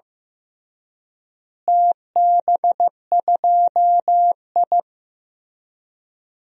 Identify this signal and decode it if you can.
Morse code, heard as a keyed tone, about 15 wpm, 700 Hz. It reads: TB2I
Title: DXCC-100 Call Signs - 4 Characters - Repeated 3x - 15wpm